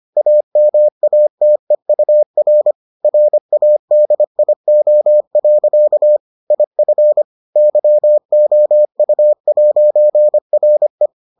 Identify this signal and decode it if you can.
Morse code, heard as a keyed tone, about 25 wpm, 600 Hz. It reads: AMATEUR RADIO. IF YOU'RE